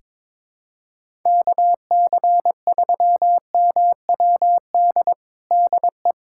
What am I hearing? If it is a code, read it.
Morse code, heard as a keyed tone, about 22 wpm, 700 Hz. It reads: KC3MWD DE
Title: USA Callsign DE Callsign 22wpm